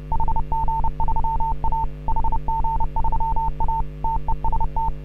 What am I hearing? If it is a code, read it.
Morse code, heard as a keyed tone, about 30 wpm, 900 Hz. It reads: HG3A HG3A TEST